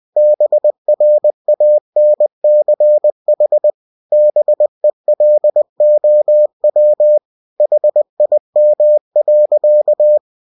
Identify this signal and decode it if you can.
Morse code, heard as a keyed tone, about 20 wpm, 600 Hz. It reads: BRANCH BELOW HIM.
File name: cw-books / winnie-the-pooh / winnie-20wpm-page-0003